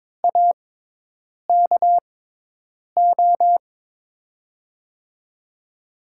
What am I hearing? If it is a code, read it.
Morse code, heard as a keyed tone, about 23 wpm, spaced as for 8 wpm, 700 Hz. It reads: AKO
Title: USA Call Signs - 6x Character Spacing - 22wpm